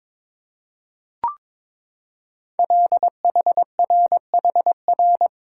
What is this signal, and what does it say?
Morse code, heard as a keyed tone, about 22 wpm, 700 Hz. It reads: LHRHR
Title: Code Groups - 5 Letter:Numbers - 22wpm